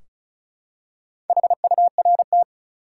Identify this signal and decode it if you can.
Morse code, heard as a keyed tone, about 34 wpm, 700 Hz. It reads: HURT
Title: Four Letter Words - 35wpm